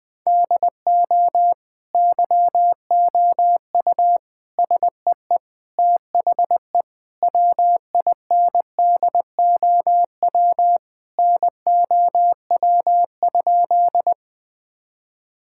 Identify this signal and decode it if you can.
Morse code, heard as a keyed tone, about 19 wpm, 700 Hz. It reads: DO YOU SEE THE WINDOW NOW?
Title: Sentences from Top 1600 Words 20wpm